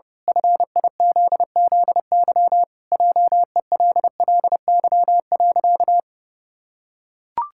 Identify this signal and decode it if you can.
Morse code, heard as a keyed tone, about 30 wpm, 700 Hz. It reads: FIZZY JELLY.
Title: Pangram Challenge v2 - 30wpm